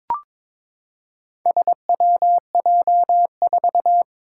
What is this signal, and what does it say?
Morse code, heard as a keyed tone, about 22 wpm, 700 Hz. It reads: SWJ4
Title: Four Challenging Letter-Number Combinations 22wpm